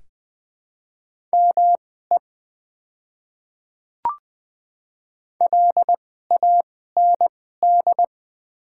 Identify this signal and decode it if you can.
Morse code, heard as a keyed tone, about 20 wpm, spaced as for 15 wpm, 700 Hz. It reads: ME LAND
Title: 2x Character Spacing - Top 300 Words - 20wpm